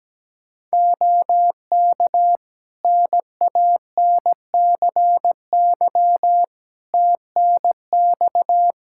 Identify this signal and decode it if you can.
Morse code, heard as a keyed tone, about 17 wpm, 700 Hz. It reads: OK NANCY TNX
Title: Practice QSO - Exchange 4 - Ending - 17wpm